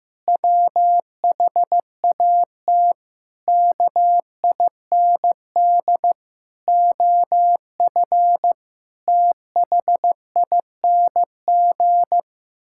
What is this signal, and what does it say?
Morse code, heard as a keyed tone, about 15 wpm, 700 Hz. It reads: WHAT KIND OF THING
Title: Sentences from Top 1000 Words - Review Part 6 - 15wpm